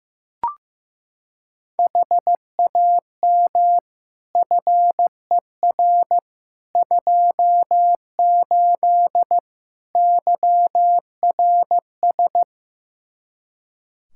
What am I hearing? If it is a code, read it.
Morse code, heard as a keyed tone, about 15 wpm, 700 Hz. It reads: HAM FER 28 YRS
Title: Practice QSO - Exchange 3 - More Info - 15wpm